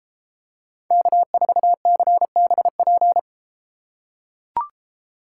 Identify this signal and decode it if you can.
Morse code, heard as a keyed tone, about 32 wpm, 700 Hz. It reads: K4CBP